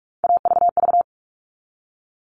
Morse code, heard as a keyed tone, about 44 wpm, 700 Hz. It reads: AVV
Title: All Cut Numbers - 3-Digit 45wpm